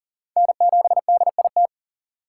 Morse code, heard as a keyed tone, about 40 wpm, 700 Hz. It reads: N7DIT
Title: USA Call Signs - 5 Characters - 2x Character Spacing - 40wpm